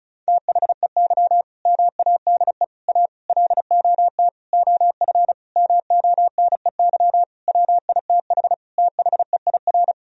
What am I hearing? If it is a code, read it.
Morse code, heard as a keyed tone, about 34 wpm, 700 Hz. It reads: THEY MADE A LOT OF MONEY WITH THEIR
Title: Sentences from Top 1000 Words - Review Part 4 - 35wpm